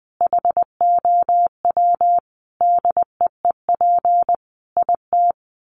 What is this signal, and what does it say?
Morse code, heard as a keyed tone, about 20 wpm, 700 Hz. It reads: HOW DEEP IT
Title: Sentences from Top 1100 Words 20wpm